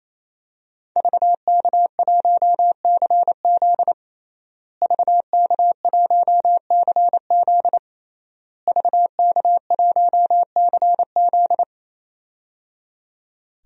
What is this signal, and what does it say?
Morse code, heard as a keyed tone, about 28 wpm, 700 Hz. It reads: VK1CZ VK1CZ VK1CZ
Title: DXCC-100 Call Signs - 5 Characters - Repeated 3x - 28wpm